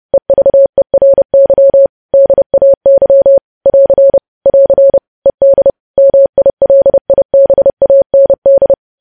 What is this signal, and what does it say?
Morse code, heard as a keyed tone, about 30 wpm, 550 Hz. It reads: EVERY DAY <AR> <AR> ED MILIBAND
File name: podcast / News headlines-1x-30x30